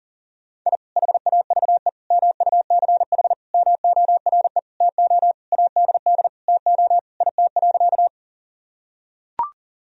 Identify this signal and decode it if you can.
Morse code, heard as a keyed tone, about 40 wpm, 700 Hz. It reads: I HAVE MUCH MORE TO ADD TO IT.
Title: Sentences from Top 1000 Words - Review Part 1 - 40wpm